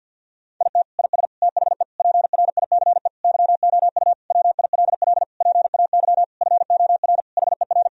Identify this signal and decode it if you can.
Morse code, heard as a keyed tone, about 50 wpm, 700 Hz. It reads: IT IS THE PRICE YOU WILL PAY FOR HER
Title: Sentences from Top 1000 Words - Review Part 4 - 50wpm